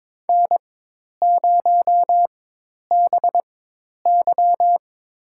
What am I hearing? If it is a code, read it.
Morse code, heard as a keyed tone, about 22 wpm, spaced as for 10 wpm, 700 Hz. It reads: N0BY
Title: USA Call Signs - 4 Characters - 4x Character Spacing - 22wpm